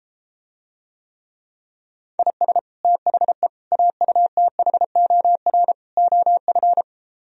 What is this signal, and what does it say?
Morse code, heard as a keyed tone, about 33 wpm, 700 Hz. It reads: IS THE AUTHOR OF